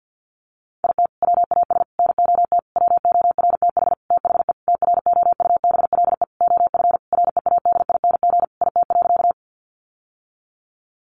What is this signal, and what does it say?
Morse code, heard as a keyed tone, about 50 wpm, 700 Hz. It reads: IT WAS NOT WORTH THE TROUBLE OF READING IT.
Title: Sentences from Top 1200 Words 50wpm